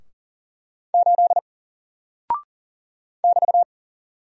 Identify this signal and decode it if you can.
Morse code, heard as a keyed tone, about 40 wpm, 700 Hz. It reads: 8 <BT>
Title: Single Character (Letter:Number:Punctuation:Prosign) Alt 40wpm